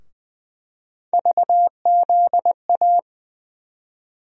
Morse code, heard as a keyed tone, about 20 wpm, 700 Hz. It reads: VZA
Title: USA Call Sign Postfixes - 3 Letters 20wpm